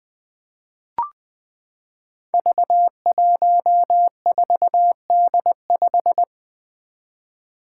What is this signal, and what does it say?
Morse code, heard as a keyed tone, about 20 wpm, 700 Hz. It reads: V14D5
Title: Five Challenging Letter-Number Combinations 20wpm